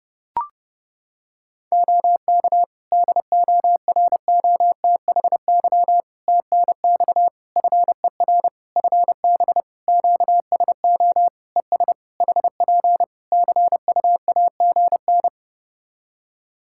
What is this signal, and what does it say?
Morse code, heard as a keyed tone, about 32 wpm, 700 Hz. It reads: OK DOROTHY TNX FER FB QSO ES HP CUAGN
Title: Practice QSO - Exchange 4 - Ending - 30wpm